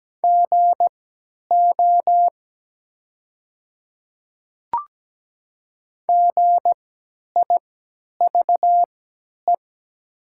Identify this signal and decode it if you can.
Morse code, heard as a keyed tone, about 17 wpm, spaced as for 10 wpm, 700 Hz. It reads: GO GIVE
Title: 3x Character Spacing - Top 200 Words - 17wpm